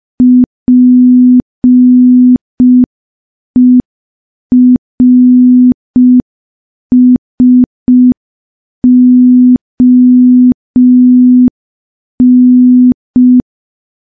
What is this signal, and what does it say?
Morse code, heard as a keyed tone, about 5 wpm, 250 Hz. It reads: PERSON
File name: morse / morse-training-05WPM